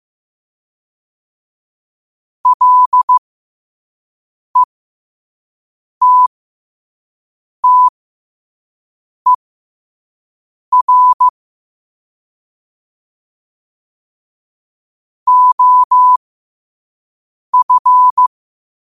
Morse code, heard as a keyed tone, about 15 wpm, spaced as for 5 wpm, 1000 Hz. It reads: LETTER OF
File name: qotdfiles / 5wpm